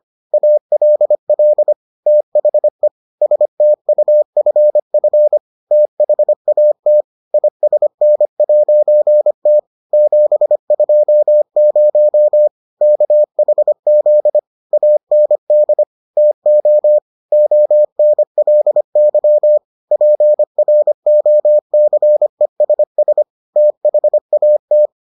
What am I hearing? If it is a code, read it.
Morse code, heard as a keyed tone, about 25 wpm, 600 Hz. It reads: ALL THE STUFF THAT ISN'T 720 KHZ AND TO ONLY PROCESS THAT